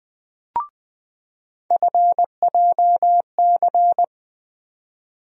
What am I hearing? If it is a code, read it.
Morse code, heard as a keyed tone, about 21 wpm, 700 Hz. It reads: FJC
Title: USA Call Sign Postfixes - 3 Letters 20wpm